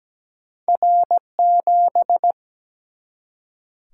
Morse code, heard as a keyed tone, about 17 wpm, 700 Hz. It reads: R7
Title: Single-Letter Single-Number 17wpm